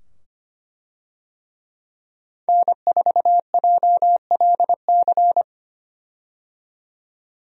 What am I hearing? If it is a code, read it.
Morse code, heard as a keyed tone, about 25 wpm, 700 Hz. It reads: N4JLC
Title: USA Call Signs - 5 Characters - 25wpm